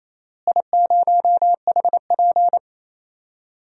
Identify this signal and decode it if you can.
Morse code, heard as a keyed tone, about 28 wpm, 700 Hz. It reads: I0HP
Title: DXCC-100 Call Signs - 4 Characters - Repeated 3x - 28wpm